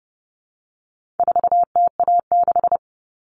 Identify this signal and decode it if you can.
Morse code, heard as a keyed tone, about 30 wpm, 700 Hz. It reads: 4TA6
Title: All Cut Numbers - 4-Digit 30wpm